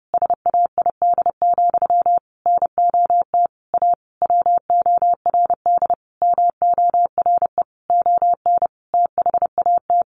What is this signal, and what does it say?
Morse code, heard as a keyed tone, about 28 wpm, 700 Hz. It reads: SAID, NOT A WORD MORE ON THAT
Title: Sentences from Top 800 Words 30wpm